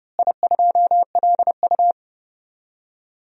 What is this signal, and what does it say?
Morse code, heard as a keyed tone, about 32 wpm, 700 Hz. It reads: I2LU
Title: DXCC-100 Call Signs - 4 Characters - Repeated 2x - 30wpm